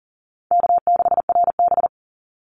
Morse code, heard as a keyed tone, about 40 wpm, 700 Hz. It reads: K6RB
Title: Easy Call Signs Repeated 3x - 40wpm